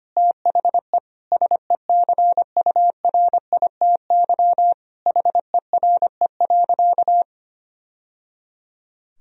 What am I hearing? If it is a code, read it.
Morse code, heard as a keyed tone, about 25 wpm, 700 Hz. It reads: THE SECURITY HERE.
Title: Sentences from Top 1000 Words - Review Part 4 - 25wpm